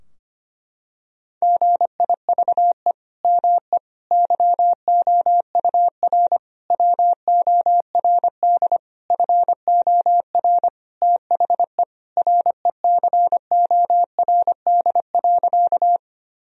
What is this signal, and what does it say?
Morse code, heard as a keyed tone, about 25 wpm, 700 Hz. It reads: GIVE ME YOUR WORD FOR THE RECORD.
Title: Sentences from Top 1000 Words - Review Part 3 - 25wpm